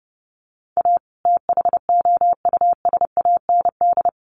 Morse code, heard as a keyed tone, about 30 wpm, 700 Hz. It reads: A THOUSAND
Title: Sentences from Top 600 Words 30wpm